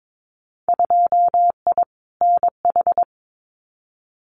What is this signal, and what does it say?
Morse code, heard as a keyed tone, about 22 wpm, 700 Hz. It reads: 2I NH
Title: Winter Field Day Exchanges Encore 22wpm